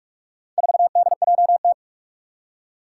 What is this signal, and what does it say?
Morse code, heard as a keyed tone, about 42 wpm, 700 Hz. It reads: 4DJT
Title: Four Challenging Letter-Number Combinations 45wpm